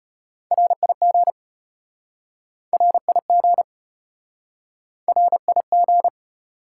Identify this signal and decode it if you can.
Morse code, heard as a keyed tone, about 36 wpm, 700 Hz. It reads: RIG RIG RIG